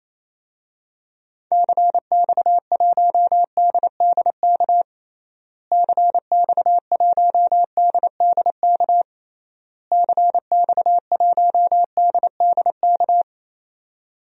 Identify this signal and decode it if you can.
Morse code, heard as a keyed tone, about 28 wpm, 700 Hz. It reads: CX1DDK CX1DDK CX1DDK
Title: DXCC-100 Call Signs - 6 Characters - Repeated 3x - 28wpm